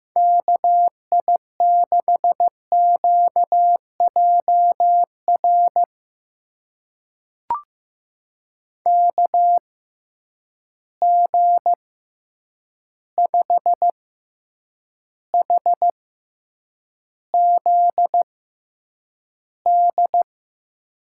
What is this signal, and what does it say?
Morse code, heard as a keyed tone, about 15 wpm, spaced as for 8 wpm, 700 Hz. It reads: KI6QJR KG5HZD
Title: USA Call Signs - 6 Characters - 6x Character Spacing - 15wpm